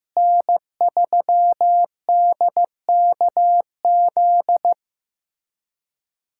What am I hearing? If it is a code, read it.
Morse code, heard as a keyed tone, about 15 wpm, 700 Hz. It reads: N3DKZ
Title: USA Callsign DE Callsign - 5 Characters - 15wpm